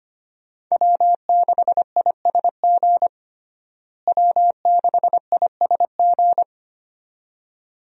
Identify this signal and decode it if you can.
Morse code, heard as a keyed tone, about 25 wpm, 700 Hz. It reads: W6ISG W6ISG
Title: USA Call Signs - 5 Characters - Repeated 2x - 25wpm